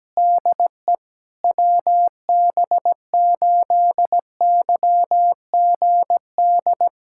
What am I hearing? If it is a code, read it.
Morse code, heard as a keyed tone, about 17 wpm, 700 Hz. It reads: DE WB8YGD
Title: USA Callsign DE Callsign 17wpm